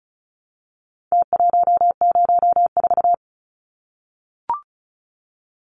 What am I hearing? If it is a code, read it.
Morse code, heard as a keyed tone, about 34 wpm, 700 Hz. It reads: T104
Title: Numbers - Contest Style - 4-Digit 35wpm